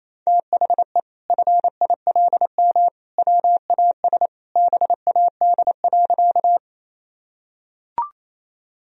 Morse code, heard as a keyed tone, about 27 wpm, 700 Hz. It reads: THE FILM WAS BAD.
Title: Sentences from Top 1000 Words - Review Part 3 - 28wpm